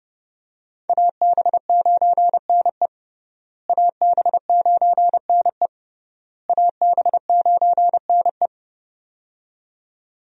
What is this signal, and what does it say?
Morse code, heard as a keyed tone, about 29 wpm, 700 Hz. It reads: AB9NE AB9NE AB9NE
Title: USA Call Signs - 5 Characters - Repeated 3x - 30wpm